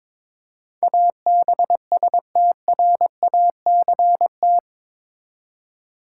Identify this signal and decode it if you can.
Morse code, heard as a keyed tone, about 22 wpm, 700 Hz. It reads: ABSTRACT
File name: practice / 22wpm / Eight Letter Words - 22wpm